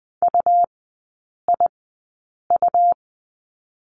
Morse code, heard as a keyed tone, about 20 wpm, spaced as for 8 wpm, 700 Hz. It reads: UIU
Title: 100 Lesson 17e - Letter u - Three Characters 20wpm